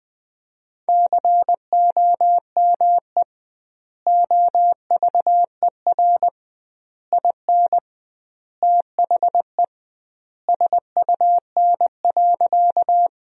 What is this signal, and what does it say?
Morse code, heard as a keyed tone, about 20 wpm, 700 Hz. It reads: COME OVER IN THE SUN.
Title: 2x Word-Spacing - Sentences from Top 300 Words - 20wpm